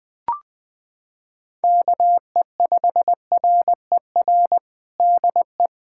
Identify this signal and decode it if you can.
Morse code, heard as a keyed tone, about 21 wpm, 700 Hz. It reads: KE5RER DE
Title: USA Callsign DE Callsign 20wpm